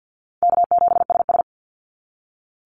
Morse code, heard as a keyed tone, about 50 wpm, 700 Hz. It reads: K7SS
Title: Easy Call Signs - 50wpm